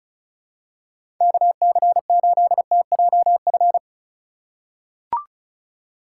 Morse code, heard as a keyed tone, about 36 wpm, 700 Hz. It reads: KC8TJF
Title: USA Call Signs - 6 Characters - 2x Character Spacing - 35wpm